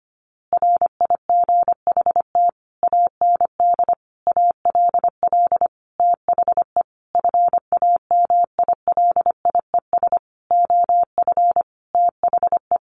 Morse code, heard as a keyed tone, about 25 wpm, 700 Hz. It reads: RIGHT AND ALL THE FAMILIES OF THE
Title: A Sound Of Thunder 25wpm